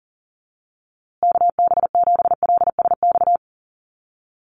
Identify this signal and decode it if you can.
Morse code, heard as a keyed tone, about 40 wpm, 700 Hz. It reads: KB7LSX
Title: USA Call Signs Repeated 2x 40wpm